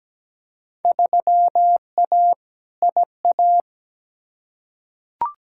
Morse code, heard as a keyed tone, about 17 wpm, 700 Hz. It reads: 3A IA
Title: Field Day Exchanges Encore - 17wpm